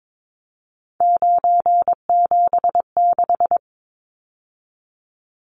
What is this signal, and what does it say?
Morse code, heard as a keyed tone, about 22 wpm, 700 Hz. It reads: 976
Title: Numbers - 3-Digit 22wpm